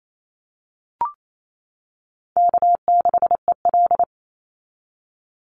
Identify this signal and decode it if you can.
Morse code, heard as a keyed tone, about 28 wpm, 700 Hz. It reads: K6EL
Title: NA SOTA Chaser Callsigns 28wpm